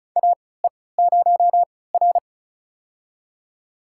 Morse code, heard as a keyed tone, about 36 wpm, spaced as for 20 wpm, 700 Hz. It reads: AE0R
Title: USA Call Signs - 4 Characters - 3x Character Spacing - 35wpm